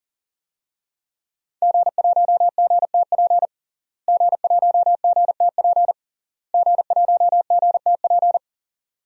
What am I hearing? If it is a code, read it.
Morse code, heard as a keyed tone, about 40 wpm, 700 Hz. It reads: G1GTP G1GTP G1GTP
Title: DXCC-100 Call Signs - 5 Characters - Repeated 3x - 40wpm